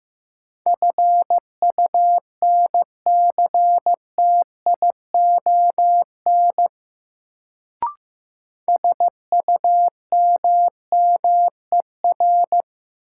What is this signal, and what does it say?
Morse code, heard as a keyed tone, about 15 wpm, 700 Hz. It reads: FUNCTION SUMMER
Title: Sentences from Top 1100 Words - New Word Drill Alt - 15wpm